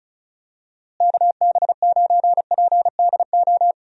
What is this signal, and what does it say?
Morse code, heard as a keyed tone, about 34 wpm, 700 Hz. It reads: KB9PDO
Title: USA Call Signs - 6 Characters - Repeated 3x - 35wpm